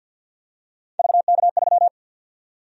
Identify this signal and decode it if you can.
Morse code, heard as a keyed tone, about 52 wpm, 700 Hz. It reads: VK3
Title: USA Call Sign Prefixes - Repeated 2x - 50wpm